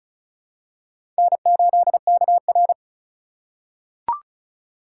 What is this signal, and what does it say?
Morse code, heard as a keyed tone, about 36 wpm, 700 Hz. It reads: N8KR
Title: USA Call Signs - 4x Character Spacing - 35wpm